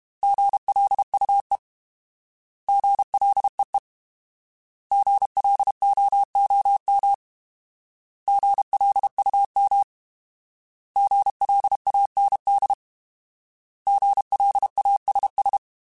Morse code, heard as a keyed tone, about 32 wpm, 800 Hz. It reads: GLUE GLEE GLOOM GLUM GLAND GLASS
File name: Alphabetical / Alphabetical mp3s / Gs @30